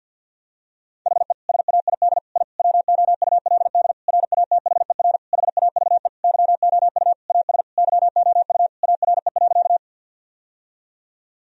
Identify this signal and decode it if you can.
Morse code, heard as a keyed tone, about 50 wpm, 700 Hz. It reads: HE SAID I WOULD RATHER HAVE YOU AS YOU ARE.
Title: Sentences from Top 1000 Words - Review Part 4 - 50wpm